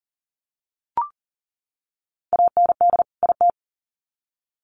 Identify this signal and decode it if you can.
Morse code, heard as a keyed tone, about 40 wpm, 700 Hz. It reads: AND IT
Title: Sets of 2 Words 40wpm